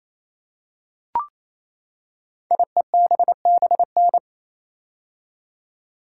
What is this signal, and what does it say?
Morse code, heard as a keyed tone, about 27 wpm, 700 Hz. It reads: IEBBN
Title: Code Groups - 5 Letters - 28wpm